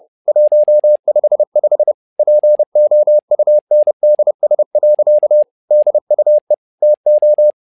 Morse code, heard as a keyed tone, about 30 wpm, 600 Hz. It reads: 155 POUNDS. DUE TO
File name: learning center / tower trouble / tower trouble 30wpm